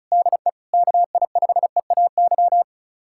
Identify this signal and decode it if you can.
Morse code, heard as a keyed tone, about 36 wpm, 700 Hz. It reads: DE KI5EAY
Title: USA Callsign DE Callsign - 6 Characters - 35wpm